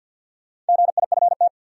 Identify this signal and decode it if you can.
Morse code, heard as a keyed tone, about 47 wpm, 700 Hz. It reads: GIFT
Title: Sentences from Top 900 Words - New Word Drill - 50wpm